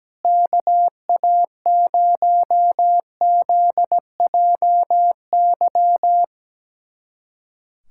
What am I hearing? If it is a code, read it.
Morse code, heard as a keyed tone, about 17 wpm, 700 Hz. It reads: KA0ZJY
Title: USA Callsign DE Callsign 17wpm